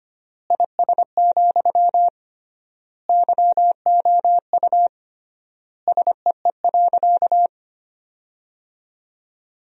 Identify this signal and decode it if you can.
Morse code, heard as a keyed tone, about 25 wpm, 700 Hz. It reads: IS, YOU SEE.
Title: 3x Word-Spacing - Sentences from Top 100 Words - 25wpm